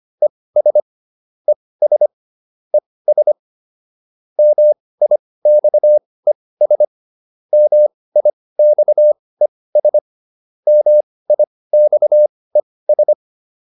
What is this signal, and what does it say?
Morse code, heard as a keyed tone, about 25 wpm, spaced as for 18 wpm, 600 Hz. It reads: ES ES ES MIXES MIXES MIXES